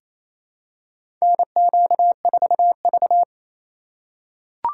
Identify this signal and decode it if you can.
Morse code, heard as a keyed tone, about 27 wpm, 700 Hz. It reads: NQ4V
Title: USA Call Signs - 4 Characters - 4x Character Spacing - 28wpm